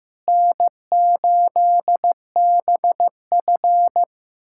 Morse code, heard as a keyed tone, about 15 wpm, 700 Hz. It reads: N8BF
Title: USA Callsign DE Callsign - 4 Characters - 15wpm